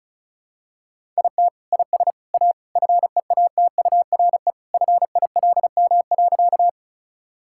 Morse code, heard as a keyed tone, about 34 wpm, 700 Hz. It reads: IT IS A FEATURE FILM.
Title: Sentences from Top 1000 Words - Review Part 5 - 35wpm